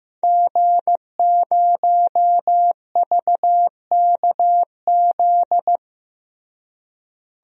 Morse code, heard as a keyed tone, about 16 wpm, 700 Hz. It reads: G0VKZ
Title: DXCC-100 Call Signs - 5 Characters - Repeated 2x - 15wpm